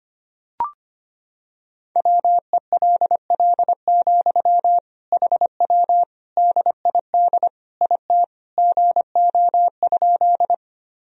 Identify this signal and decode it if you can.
Morse code, heard as a keyed tone, about 25 wpm, 700 Hz. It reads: WELL, HW DID IT GO?
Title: Sentences from Top 200 Words and Abbreviations 25wpm